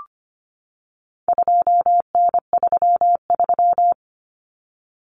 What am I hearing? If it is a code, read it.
Morse code, heard as a keyed tone, about 25 wpm, 700 Hz. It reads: 2N33
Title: Common Cut Numbers - 4-Digit 25wpm